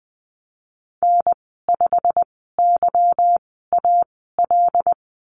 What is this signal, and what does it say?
Morse code, heard as a keyed tone, about 20 wpm, spaced as for 14 wpm, 700 Hz. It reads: N5YAL
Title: 161 Lesson 26g - Letter g - Call Signs 20wpm